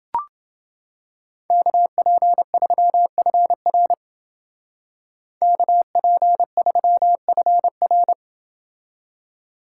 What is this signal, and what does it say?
Morse code, heard as a keyed tone, about 28 wpm, 700 Hz. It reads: KP3FR KP3FR